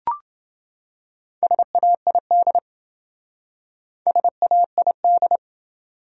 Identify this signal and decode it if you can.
Morse code, heard as a keyed tone, about 27 wpm, 700 Hz. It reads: SAID SAID